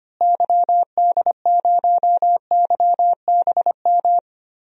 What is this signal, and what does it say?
Morse code, heard as a keyed tone, about 25 wpm, 700 Hz. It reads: YD0YBM
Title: DXCC-100 Call Signs - 6 Characters - Repeated 3x - 25wpm